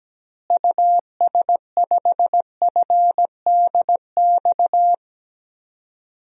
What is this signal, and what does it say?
Morse code, heard as a keyed tone, about 17 wpm, 700 Hz. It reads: US5FDX
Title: DXCC-100 Call Signs - 6 Characters - Repeated 2x - 17wpm